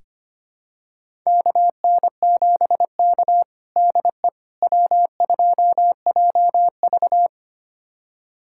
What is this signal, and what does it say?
Morse code, heard as a keyed tone, about 25 wpm, 700 Hz. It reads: KN7K DE W2JV
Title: USA Callsign DE Callsign - 4 Characters - 25wpm